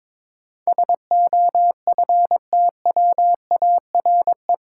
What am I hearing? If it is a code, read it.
Morse code, heard as a keyed tone, about 22 wpm, 700 Hz. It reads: SOFTWARE
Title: Sentences from Top 1000 Words - Review Part 3 - New Word Drill - 22wpm